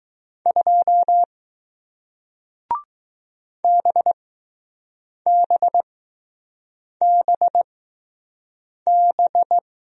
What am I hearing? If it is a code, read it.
Morse code, heard as a keyed tone, about 20 wpm, spaced as for 6 wpm, 700 Hz. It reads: 2 BBBB